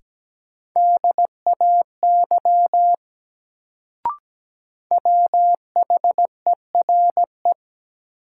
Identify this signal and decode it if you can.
Morse code, heard as a keyed tone, about 17 wpm, 700 Hz. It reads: DAY WHERE
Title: Sentences from Top 200 Words - New Word Drill - 17wpm